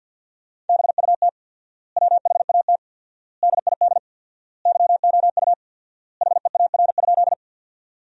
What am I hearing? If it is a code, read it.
Morse code, heard as a keyed tone, about 52 wpm, 700 Hz. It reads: BUT WHAT DID YOU HEAR?
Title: 4x Word-Spacing - Sentences from Top 300 Words - 50wpm